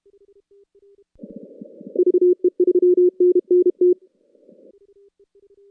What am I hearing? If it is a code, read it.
Morse code, heard as a keyed tone, about 29 wpm, 400 Hz. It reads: VE3NNT
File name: contests / CW-SS-k5tr- -2100